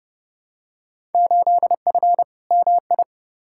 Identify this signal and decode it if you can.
Morse code, heard as a keyed tone, about 30 wpm, 700 Hz. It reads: 8F MI
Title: Field Day Exchanges Encore - 30wpm